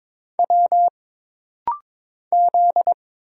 Morse code, heard as a keyed tone, about 22 wpm, 700 Hz. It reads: W Z
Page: This rapid-fire timing is geared to push your ICR (Instant Character Recognition) skills.